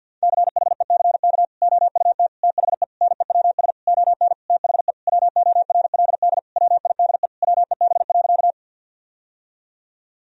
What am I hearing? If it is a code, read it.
Morse code, heard as a keyed tone, about 50 wpm, 700 Hz. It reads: CHECK OUT THE NEWS ON THE WORLD WIDE WEB.
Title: Sentences from Top 1000 Words - Review Part 4 - 50wpm